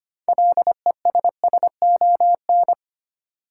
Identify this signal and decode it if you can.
Morse code, heard as a keyed tone, about 25 wpm, 700 Hz. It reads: LESSON
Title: Sentences from Top 1200 Words - New Word Drill Alt - 25wpm